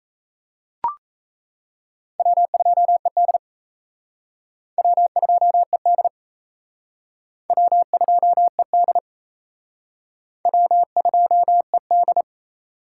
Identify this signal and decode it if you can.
Morse code, heard as a keyed tone, about 34 wpm, 700 Hz. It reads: W2ED W2ED W2ED W2ED
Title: Speed Racing - USA 1x2 Call Signs 28wpm